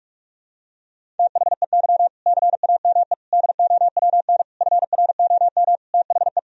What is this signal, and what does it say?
Morse code, heard as a keyed tone, about 46 wpm, 700 Hz. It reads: THEY CAME DOWN FROM THE
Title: Sentences from Top 1000 Words - Review Part 1 - 45wpm